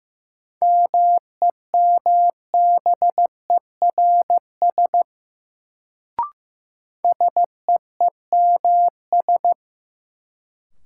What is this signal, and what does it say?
Morse code, heard as a keyed tone, about 15 wpm, 700 Hz. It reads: MEMBERS SEEMS
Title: Sets of 3 Words - New Word Drill - 15wpm